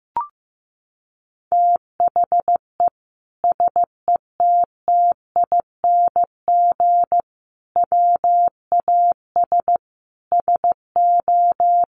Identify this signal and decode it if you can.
Morse code, heard as a keyed tone, about 15 wpm, 700 Hz. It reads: THE SETTING WAS SO
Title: Sentences from Top 1300 Words 15wpm